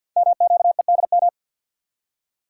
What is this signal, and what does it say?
Morse code, heard as a keyed tone, about 48 wpm, 700 Hz. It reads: MQEDM
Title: Code Groups - 5 Letter:Numbers - 50wpm